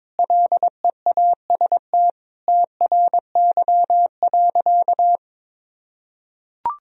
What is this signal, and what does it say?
Morse code, heard as a keyed tone, about 22 wpm, 700 Hz. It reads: LEAST TRY.
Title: Sentences from Top 1000 Words - Review Part 1 - 22wpm